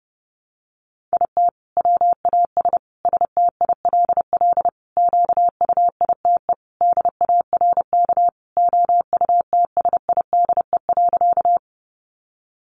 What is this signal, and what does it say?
Morse code, heard as a keyed tone, about 30 wpm, 700 Hz. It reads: IT WAS STILL QUITE DARK OUTSIDE.
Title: Sentences from Top 900 Words 30wpm